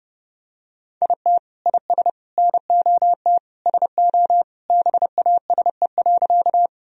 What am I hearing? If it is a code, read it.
Morse code, heard as a keyed tone, about 30 wpm, 700 Hz. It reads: IT IS NOT SO BASE.
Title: Sentences from Top 1000 Words - Review Part 1 - 30wpm